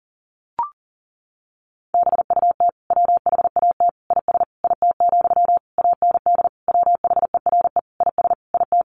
Morse code, heard as a keyed tone, about 40 wpm, 700 Hz. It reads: BUT WHAT IS IT, AND WHERE IS IT
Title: Sentences from Top 300 Words 40wpm